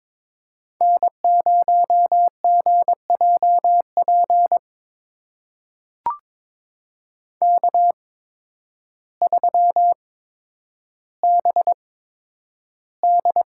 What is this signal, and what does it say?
Morse code, heard as a keyed tone, about 22 wpm, spaced as for 10 wpm, 700 Hz. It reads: N0GJP K3BD
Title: USA Call Signs - 5 Characters - 8x Character Spacing - 22wpm